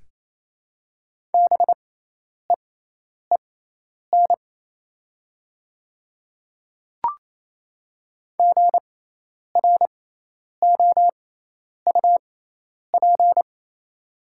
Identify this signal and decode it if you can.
Morse code, heard as a keyed tone, about 28 wpm, spaced as for 10 wpm, 700 Hz. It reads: BEEN GROUP
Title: 6x Character Spacing - Top 300 Words - 28wpm